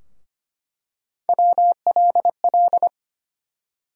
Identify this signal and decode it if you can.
Morse code, heard as a keyed tone, about 25 wpm, 700 Hz. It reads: WLL
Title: USA Call Sign Postfixes - 3 Letters 25wpm